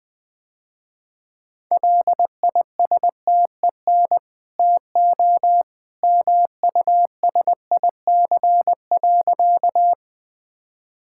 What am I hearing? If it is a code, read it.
Morse code, heard as a keyed tone, about 20 wpm, 700 Hz. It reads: LISTEN TO MUSIC.
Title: Easy Sentences with Hints - 20wpm